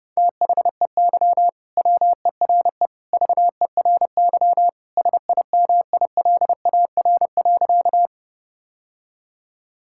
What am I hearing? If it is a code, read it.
Morse code, heard as a keyed tone, about 30 wpm, 700 Hz. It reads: THEY WERE VERY SIMILAR.
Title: Sentences from Top 1000 Words - Review Part 5 - 30wpm